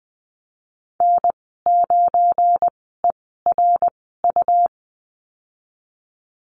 Morse code, heard as a keyed tone, about 20 wpm, spaced as for 14 wpm, 700 Hz. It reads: N9ERU
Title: 245 Lesson 39g - Letter z - Call Signs 20wpm